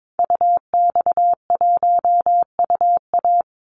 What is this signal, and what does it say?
Morse code, heard as a keyed tone, about 22 wpm, 700 Hz. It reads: UX1UA
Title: Call Signs - 22wpm